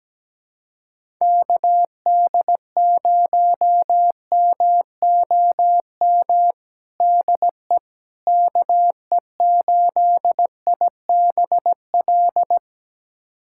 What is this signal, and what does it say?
Morse code, heard as a keyed tone, about 17 wpm, 700 Hz. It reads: KD0MOM DE KE8IBL
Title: USA Callsign DE Callsign - 6 Characters - 17wpm